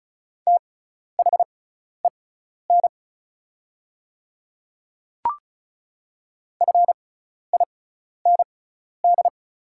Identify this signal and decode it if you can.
Morse code, heard as a keyed tone, about 36 wpm, spaced as for 12 wpm, 700 Hz. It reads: THEN FIND
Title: 6x Character Spacing - Top 300 Words - 35wpm